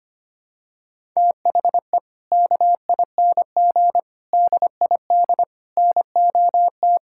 Morse code, heard as a keyed tone, about 25 wpm, 700 Hz. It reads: THE KING DID NOT
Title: Sentences from Top 1000 Words - Review Part 5 - 25wpm